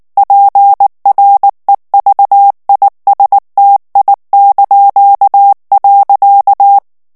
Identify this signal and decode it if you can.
Morse code, heard as a keyed tone, about 19 wpm, 800 Hz. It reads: PREVISTI).
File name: morse mp3 / ITALIEN texte clair 2 vitesse 18 mots minute